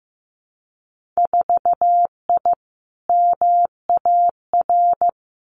Morse code, heard as a keyed tone, about 15 wpm, 700 Hz. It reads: 4I MAR
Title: Winter Field Day Exchanges Encore 15wpm